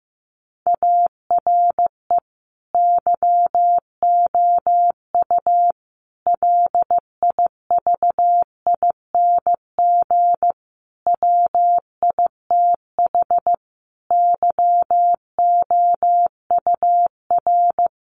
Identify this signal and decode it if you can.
Morse code, heard as a keyed tone, about 15 wpm, 700 Hz. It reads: ARE YOU LIVING WITH YOUR
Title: Sentences from Top 1100 Words 15wpm